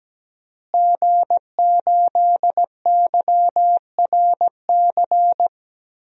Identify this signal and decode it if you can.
Morse code, heard as a keyed tone, about 17 wpm, 700 Hz. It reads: G8YRC
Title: DXCC-100 Call Signs - 5 Characters - Repeated 2x - 17wpm